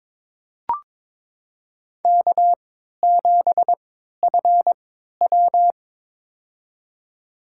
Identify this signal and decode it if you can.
Morse code, heard as a keyed tone, about 21 wpm, spaced as for 12 wpm, 700 Hz. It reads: K7FW
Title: USA Call Signs - 4 Characters - 3x Character Spacing - 22wpm